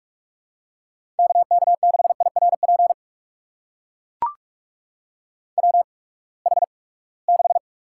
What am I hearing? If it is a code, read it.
Morse code, heard as a keyed tone, about 43 wpm, spaced as for 25 wpm, 700 Hz. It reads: KK6IRP WH6
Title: USA Call Signs - 8x Character Spacing - 45wpm